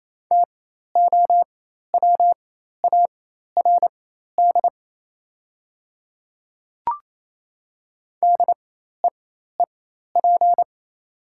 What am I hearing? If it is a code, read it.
Morse code, heard as a keyed tone, about 28 wpm, spaced as for 13 wpm, 690 Hz. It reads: TOWARD DEEP
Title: 4x Character Spacing - Top 500 Words - 28wpm